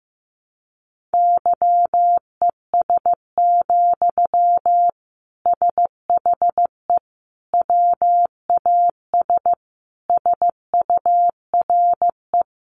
Morse code, heard as a keyed tone, about 15 wpm, 700 Hz. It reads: YES, SHE WAS SURE
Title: Sentences from Top 1100 Words 15wpm